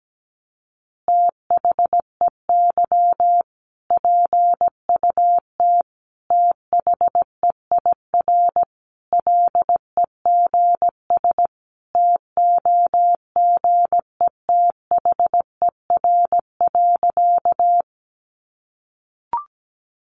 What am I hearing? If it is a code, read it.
Morse code, heard as a keyed tone, about 17 wpm, 700 Hz. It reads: THEY PUT THEIR LEGS TOGETHER.
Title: Easy Phrases 17wpm